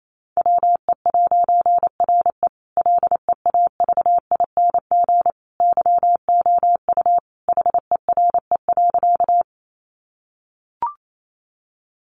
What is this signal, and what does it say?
Morse code, heard as a keyed tone, about 28 wpm, 700 Hz. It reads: WE'RE LEAVING YOU HERE.
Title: A Sound Of Thunder 28wpm